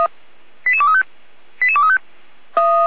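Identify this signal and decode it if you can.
Five-tone selective call (ZVEI), heard as quick runs of five tones, tones of about 70 ms. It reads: 88235 88235
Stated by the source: ZVEI five-tone selective call received by SDR.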